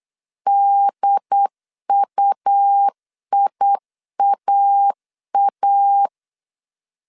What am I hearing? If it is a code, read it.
Morse code, heard as a keyed tone, about 8 wpm, 800 Hz. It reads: DUIAA
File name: CW / DET / 1 3 1500C S B 8wpm 4141